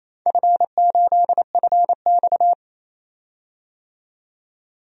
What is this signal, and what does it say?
Morse code, heard as a keyed tone, about 27 wpm, 700 Hz. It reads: F8FX
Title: DXCC-100 Call Signs - 4 Characters - 28wpm